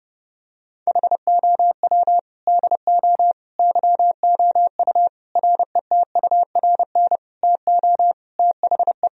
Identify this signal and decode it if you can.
Morse code, heard as a keyed tone, about 32 wpm, 700 Hz. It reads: HOW DO YOU RETURN TO THE
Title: Sentences from Top 1000 Words - Review Part 3 - 30wpm